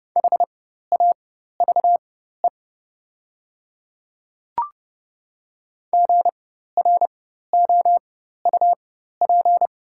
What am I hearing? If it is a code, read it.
Morse code, heard as a keyed tone, about 30 wpm, spaced as for 14 wpm, 700 Hz. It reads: HAVE GROUP
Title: 4x Character Spacing - Top 300 Words - 30wpm